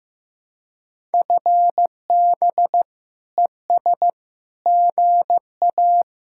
Morse code, heard as a keyed tone, about 15 wpm, 700 Hz. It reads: FB ES GA
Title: Speed-Racing Common QSO Phrases Encore - 15wpm